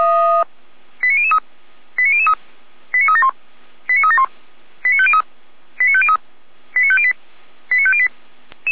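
Five-tone selective call (ZVEI), heard as quick runs of five tones, tones of about 70 ms. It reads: 89003 89003 89371 89371 89503 89503 89508 89508
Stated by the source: ZVEI five-tone selective call received on SDR, from feuerwehr-orlamuende.org fire brigade.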